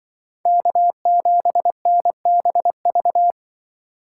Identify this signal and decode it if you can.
Morse code, heard as a keyed tone, about 23 wpm, 700 Hz. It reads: K7NBV
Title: Speed Racing - USA 1x3 Call Signs 20wpm